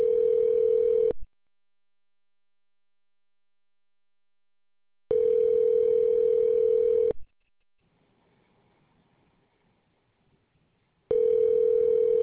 Ringback tone (North America), 440 + 480 Hz.